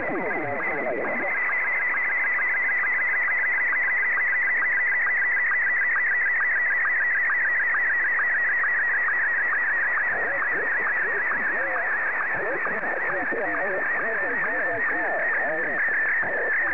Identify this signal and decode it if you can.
SSTV (slow-scan television), Martin 1, with speech.
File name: sstv / M1-VK3